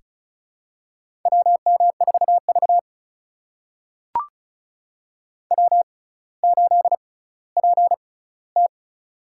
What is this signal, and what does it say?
Morse code, heard as a keyed tone, about 36 wpm, spaced as for 18 wpm, 700 Hz. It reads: WM4V W8PT
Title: USA Call Signs - 4 Characters - 6x Character Spacing - 35wpm